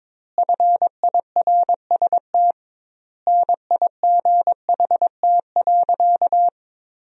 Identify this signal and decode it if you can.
Morse code, heard as a keyed tone, about 22 wpm, 700 Hz. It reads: FIRST NIGHT.
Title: 2x Word-Spacing - Sentences from Top 300 Words - 22wpm